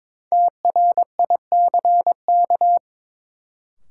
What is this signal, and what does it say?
Morse code, heard as a keyed tone, about 22 wpm, 700 Hz. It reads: TRICK
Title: Sentences from Top 1500 Words - New Word Drill - 22wpm